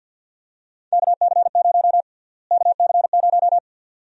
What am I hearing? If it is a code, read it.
Morse code, heard as a keyed tone, about 52 wpm, 700 Hz. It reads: KC0 KC0
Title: USA Call Sign Prefixes - Repeated 2x - 50wpm